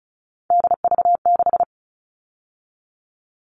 Morse code, heard as a keyed tone, about 34 wpm, 700 Hz. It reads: DV6
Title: All Cut Numbers - 3-Digit 35wpm